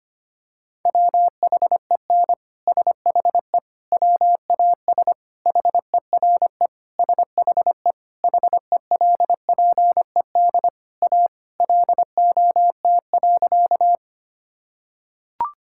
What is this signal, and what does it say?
Morse code, heard as a keyed tone, about 25 wpm, 700 Hz. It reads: WHEN SHE WAS HERE SHE HELPED A LOT.
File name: practice / 25wpm / Sentences with 3-Word Phrases - 25wpm